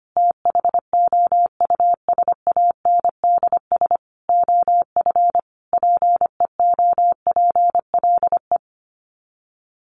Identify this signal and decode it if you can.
Morse code, heard as a keyed tone, about 25 wpm, 700 Hz. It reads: THOUSANDS OF PEOPLE
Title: Sets of 5 Words Encore2 25wpm